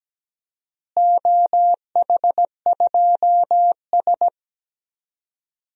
Morse code, heard as a keyed tone, about 17 wpm, 700 Hz. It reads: OH2S
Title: DXCC-100 Call Signs - 4 Characters - Repeated 2x - 17wpm